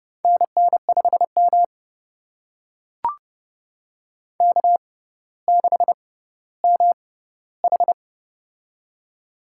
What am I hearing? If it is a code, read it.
Morse code, heard as a keyed tone, about 30 wpm, spaced as for 15 wpm, 700 Hz. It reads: NN5M K6MH
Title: USA Call Signs - 4 Characters - 6x Character Spacing - 30wpm